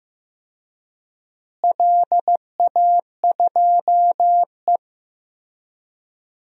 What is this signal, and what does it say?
Morse code, heard as a keyed tone, about 15 wpm, 700 Hz. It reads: LA2E
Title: DXCC-100 Call Signs - 4 Characters - Repeated 3x - 15wpm